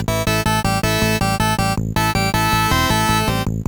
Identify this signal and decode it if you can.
There is no code.